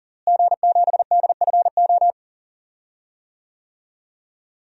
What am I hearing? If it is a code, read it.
Morse code, heard as a keyed tone, about 40 wpm, 700 Hz. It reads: G7DFO
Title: DXCC-100 Call Signs - 5 Characters - Repeated 2x - 40wpm